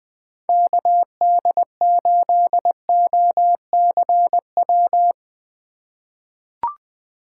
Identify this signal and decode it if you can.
Morse code, heard as a keyed tone, about 20 wpm, 700 Hz. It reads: KD8OCW
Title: 2023 POTA Hunter Call Signs - 20wpm